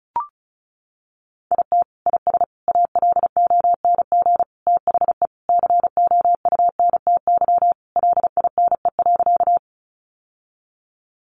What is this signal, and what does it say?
Morse code, heard as a keyed tone, about 34 wpm, 700 Hz. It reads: IT IS ALONG THE COUNTY LINE.
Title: Sentences from Top 1200 Words 35wpm